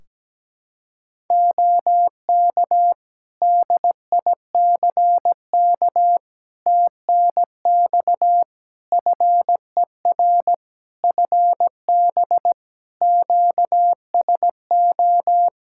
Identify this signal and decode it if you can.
Morse code, heard as a keyed tone, about 17 wpm, 700 Hz. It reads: OK DICK TNX FER FB QSO
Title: Practice QSO - Exchange 4 - Ending - 17wpm